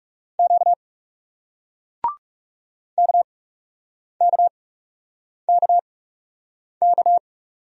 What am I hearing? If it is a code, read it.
Morse code, heard as a keyed tone, about 42 wpm, spaced as for 8 wpm, 700 Hz. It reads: Q KKKK